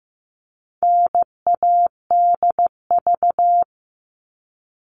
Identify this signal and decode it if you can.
Morse code, heard as a keyed tone, about 15 wpm, 700 Hz. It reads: NADV
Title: All Cut Numbers - 4-Digit 15wpm